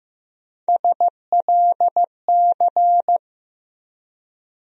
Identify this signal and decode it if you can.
Morse code, heard as a keyed tone, about 15 wpm, 700 Hz. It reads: SLC
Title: USA Call Sign Postfixes - 3 Letters 15wpm